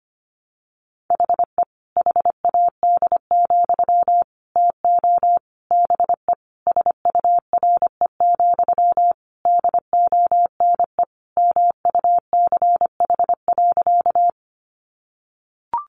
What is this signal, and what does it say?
Morse code, heard as a keyed tone, about 25 wpm, 700 Hz. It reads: HE HAD, TO BE SURE, DONE MUCH.
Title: Sentences from Top 500 Words 25wpm